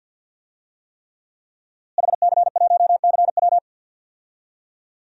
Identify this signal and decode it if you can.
Morse code, heard as a keyed tone, about 52 wpm, 700 Hz. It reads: HC1CW
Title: DXCC-100 Call Signs - 5 Characters - Repeated 3x - 50wpm